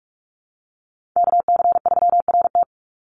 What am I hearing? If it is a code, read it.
Morse code, heard as a keyed tone, about 44 wpm, 700 Hz. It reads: KC3RT
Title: NA SOTA Chaser Callsigns 45wpm